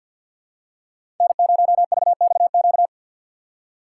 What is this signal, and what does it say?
Morse code, heard as a keyed tone, about 50 wpm, 700 Hz. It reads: N0VXQ
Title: USA Call Signs - 5 Characters - 2x Character Spacing - 50wpm